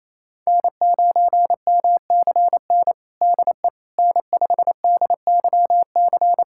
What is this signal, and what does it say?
Morse code, heard as a keyed tone, about 27 wpm, 700 Hz. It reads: N9MCN DE N5DYC
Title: USA Callsign DE Callsign - 5 Characters - 28wpm